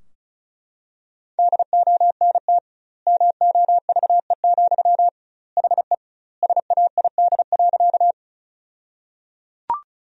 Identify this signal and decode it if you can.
Morse code, heard as a keyed tone, about 36 wpm, 700 Hz. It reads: DONT MOVE, HE SAID.
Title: 2x Word-Spacing - Sentences from Top 300 Words - 35wpm